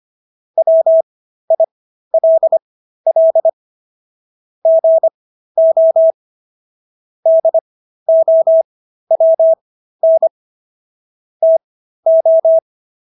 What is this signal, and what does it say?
Morse code, heard as a keyed tone, about 25 wpm, spaced as for 13 wpm, 650 Hz. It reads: WILL GO DOWN TO